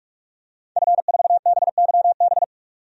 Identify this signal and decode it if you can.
Morse code, heard as a keyed tone, about 42 wpm, 700 Hz. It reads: F4BYB
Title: DXCC-100 Call Signs - 5 Characters - 45wpm